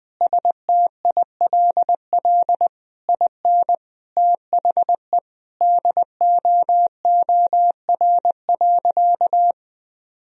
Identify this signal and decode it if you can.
Morse code, heard as a keyed tone, about 20 wpm, 700 Hz. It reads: STILL IN THE DOOR.
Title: Sentences from Top 1000 Words - Review Part 4 - 20wpm